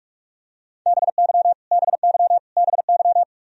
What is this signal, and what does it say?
Morse code, heard as a keyed tone, about 46 wpm, 700 Hz. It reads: BY BY BY
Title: Top 100 Words Repeated 45wpm